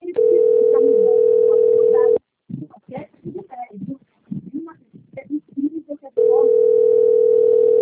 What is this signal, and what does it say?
Ringback tone (North America), 440 + 480 Hz, with speech.
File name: rg-1600-unknown- - - .32561